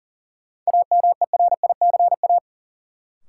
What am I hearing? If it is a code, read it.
Morse code, heard as a keyed tone, about 40 wpm, 700 Hz. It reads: AMERICA
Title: Sets of 5 Words - New Word Drill - 40wpm